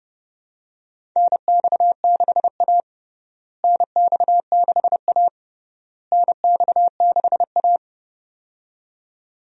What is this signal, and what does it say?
Morse code, heard as a keyed tone, about 30 wpm, 700 Hz. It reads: NX6A NX6A NX6A
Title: USA Call Signs - 4 Characters - Repeated 3x - 30wpm